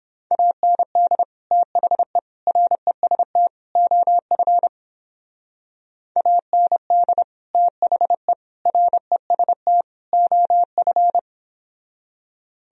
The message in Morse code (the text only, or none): AND THE REST OF AND THE REST OF